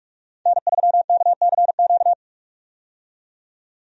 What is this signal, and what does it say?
Morse code, heard as a keyed tone, about 46 wpm, 700 Hz. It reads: N3KCQ
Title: USA Call Signs - 5 Characters - 45wpm